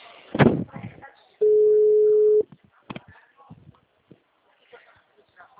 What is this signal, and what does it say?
Ringback tone (Europe, 425 Hz), with speech.